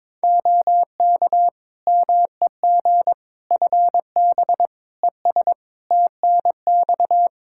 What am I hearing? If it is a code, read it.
Morse code, heard as a keyed tone, about 22 wpm, 700 Hz. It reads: OK MEG FB ES TNX
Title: Practice QSO - Exchange 2 - Info - 22wpm